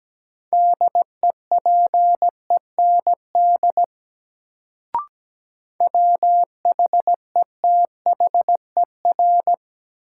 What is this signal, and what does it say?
Morse code, heard as a keyed tone, about 17 wpm, 700 Hz. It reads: DEPEND WHETHER
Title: Sentences from Top 1300 Words - New Word Drill Alt - 17wpm